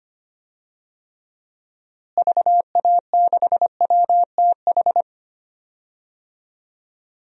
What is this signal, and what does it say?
Morse code, heard as a keyed tone, about 25 wpm, 700 Hz. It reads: VA6WTH
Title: DXCC-100 Call Signs - 6 Characters - Repeated 2x - 25wpm